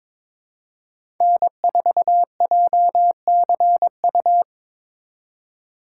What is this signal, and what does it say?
Morse code, heard as a keyed tone, about 22 wpm, 700 Hz. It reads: N4JCU
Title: USA Call Signs - 4x Character Spacing - 22wpm